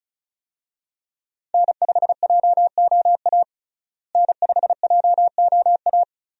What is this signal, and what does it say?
Morse code, heard as a keyed tone, about 34 wpm, 700 Hz. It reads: N5JOA N5JOA
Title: USA Call Signs - 5 Characters - Repeated 3x - 35wpm